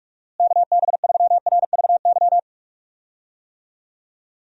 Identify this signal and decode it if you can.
Morse code, heard as a keyed tone, about 41 wpm, 700 Hz. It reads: KB3RVY
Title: USA Call Signs - 6 Characters - Repeated 3x - 45wpm